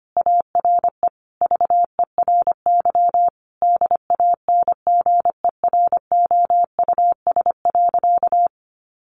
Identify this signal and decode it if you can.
Morse code, heard as a keyed tone, about 25 wpm, 700 Hz. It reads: ARE VERY DANGEROUS.
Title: Sentences from Top 1200 Words 25wpm